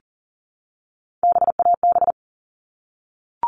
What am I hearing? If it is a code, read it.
Morse code, heard as a keyed tone, about 40 wpm, 700 Hz. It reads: BAB
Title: All Cut Numbers - 3-Digit 40wpm